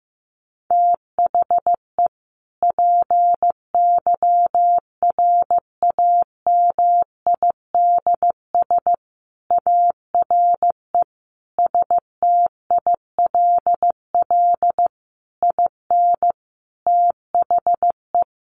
Morse code, heard as a keyed tone, about 15 wpm, 700 Hz. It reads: THE PYRAMIDS ARE STILL IN THE
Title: A Sound Of Thunder 15wpm